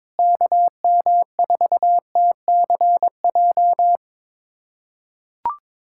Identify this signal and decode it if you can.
Morse code, heard as a keyed tone, about 22 wpm, 700 Hz. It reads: KM4TCJ
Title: USA Callsign DE Callsign - 6 Characters - 22wpm